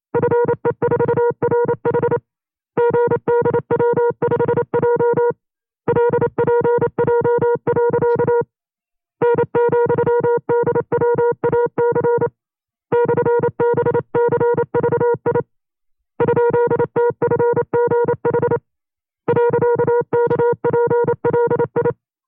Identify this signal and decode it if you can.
Morse code, heard as a keyed tone, about 28 wpm, 480 Hz. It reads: FE4RH GDW5J LPJ. N,DWAC /BCVI ?TFGH .KPLI
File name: 25WPM